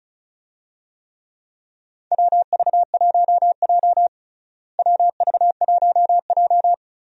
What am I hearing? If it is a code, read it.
Morse code, heard as a keyed tone, about 34 wpm, 700 Hz. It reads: WV1J WV1J
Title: USA Call Signs - 4 Characters - Repeated 2x - 35wpm